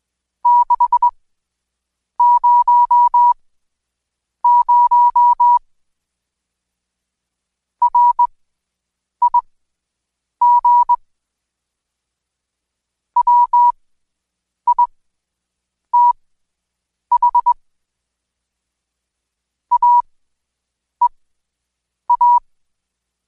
Morse code, heard as a keyed tone, about 21 wpm, spaced as for 7 wpm, 970 Hz. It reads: 600 RIG WITH AEA
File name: Code Quick / AudioDownld2 / Seg112-Day17-DayDrill 17